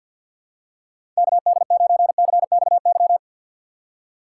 Morse code, heard as a keyed tone, about 51 wpm, 700 Hz. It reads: KD9CXY
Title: USA Call Signs - 2x Character Spacing - 50wpm